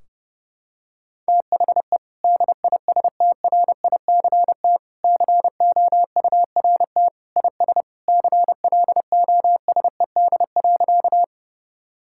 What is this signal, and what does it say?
Morse code, heard as a keyed tone, about 30 wpm, 700 Hz. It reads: THE DISTRICT COURT IS CLOSED.
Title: Sentences from Top 1600 Words 30wpm